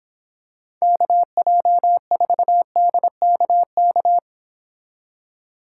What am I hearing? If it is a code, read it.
Morse code, heard as a keyed tone, about 26 wpm, 700 Hz. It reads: KJ4DKK